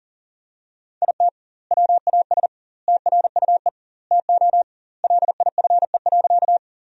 Morse code, heard as a keyed tone, about 40 wpm, 700 Hz. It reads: IT WAS TRUE TO LIFE.
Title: 2x Word-Spacing - Sentences from Top 400 Words - 40wpm